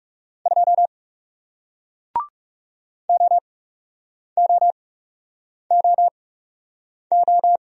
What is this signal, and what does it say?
Morse code, heard as a keyed tone, about 40 wpm, spaced as for 8 wpm, 700 Hz. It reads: 2 OOOO